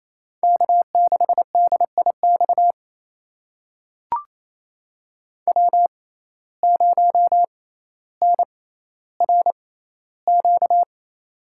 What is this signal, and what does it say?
Morse code, heard as a keyed tone, about 27 wpm, spaced as for 14 wpm, 700 Hz. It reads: K6DIX W0NRQ
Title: USA Call Signs - 6x Character Spacing - 28wpm